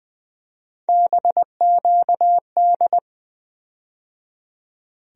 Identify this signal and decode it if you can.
Morse code, heard as a keyed tone, about 20 wpm, 700 Hz. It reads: BQD
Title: USA Call Sign Postfixes - 3 Letters 20wpm